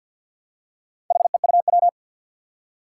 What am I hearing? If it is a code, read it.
Morse code, heard as a keyed tone, about 52 wpm, 700 Hz. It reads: HEUW
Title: Four Challenging Letter-Number Combinations 50wpm